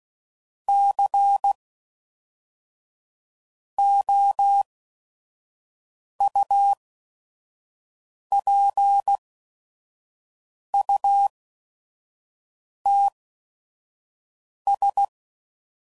Morse code, heard as a keyed tone, about 16 wpm, spaced as for 5 wpm, 800 Hz. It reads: COUPUTS
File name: CW Code Groups @5WPM